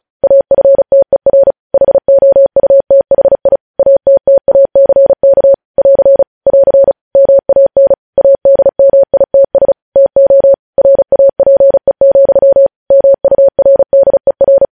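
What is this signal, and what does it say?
Morse code, heard as a keyed tone, about 36 wpm, 550 Hz. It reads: AFTER HOUTHI ATTACK <AR> <AR> MAN ADMITS TO RAPE, MURDER
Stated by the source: Morse code news headlines practice at 35.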